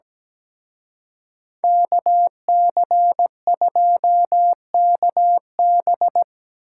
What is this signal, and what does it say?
Morse code, heard as a keyed tone, about 20 wpm, 700 Hz. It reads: KC2KB
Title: Speed Racing - USA Call Signs 17wpm